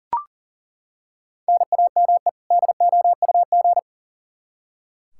Morse code, heard as a keyed tone, about 39 wpm, 700 Hz. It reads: NAME DOUG
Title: Practice QSO - Exchange 1 - Beginning - 40wpm